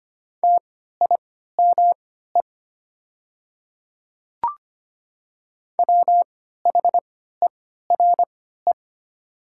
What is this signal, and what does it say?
Morse code, heard as a keyed tone, about 25 wpm, spaced as for 14 wpm, 690 Hz. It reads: TIME WHERE
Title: 3x Character Spacing - Top 200 Words - 25wpm